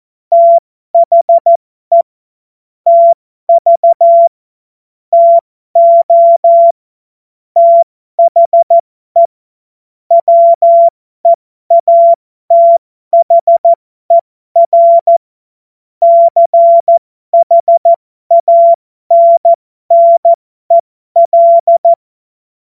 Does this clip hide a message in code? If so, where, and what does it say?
Yes: Morse code from 0.3 to 21.9 s, heard as a keyed tone, about 14 wpm, spaced as for 12 wpm, 680 Hz. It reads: THE TV TO THE WEATHER CHANNEL